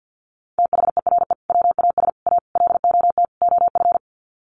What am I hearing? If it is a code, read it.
Morse code, heard as a keyed tone, about 50 wpm, 700 Hz. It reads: THERE WAS A LOT OF
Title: Sets of 5 Words 50wpm